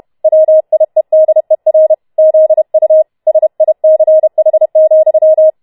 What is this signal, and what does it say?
Morse code, heard as a keyed tone, about 31 wpm, 630 Hz. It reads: WIEDER ZU SICH,